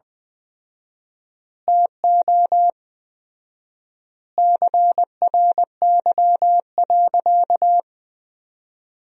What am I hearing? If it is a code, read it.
Morse code, heard as a keyed tone, about 20 wpm, 700 Hz. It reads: TO CRY.
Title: 4x Word-Spacing - Sentences from Top 500 Words - 20wpm